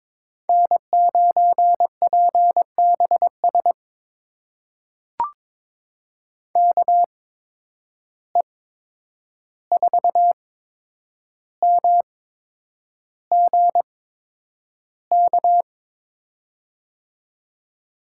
Morse code, heard as a keyed tone, about 22 wpm, spaced as for 9 wpm, 700 Hz. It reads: N9PBS KE4MGK
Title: USA Call Signs - 8x Character Spacing - 22wpm